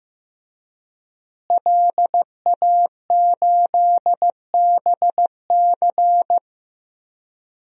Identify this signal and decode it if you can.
Morse code, heard as a keyed tone, about 15 wpm, 700 Hz. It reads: LA8BC
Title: DXCC-100 Call Signs - 5 Characters - Repeated 2x - 15wpm